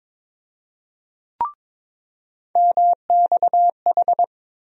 Morse code, heard as a keyed tone, about 22 wpm, 700 Hz. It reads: MXH
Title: USA Call Sign Postfixes - 3 Letters 22wpm